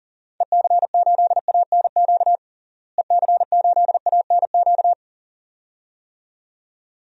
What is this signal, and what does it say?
Morse code, heard as a keyed tone, about 40 wpm, 700 Hz. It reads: EC8ANQ EC8ANQ
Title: DXCC-100 Call Signs - 6 Characters - Repeated 2x - 40wpm